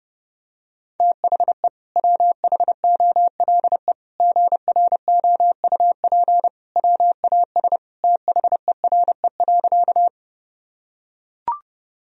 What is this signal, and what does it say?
Morse code, heard as a keyed tone, about 30 wpm, 700 Hz. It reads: THE WHOLE GROUP WAS THERE.
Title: Sentences from Top 1000 Words - Review Part 1 - 30wpm